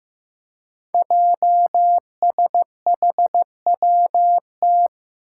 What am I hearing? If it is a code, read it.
Morse code, heard as a keyed tone, about 15 wpm, 700 Hz. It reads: JSHWT
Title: Five Challenging Letter-Number Combinations 15wpm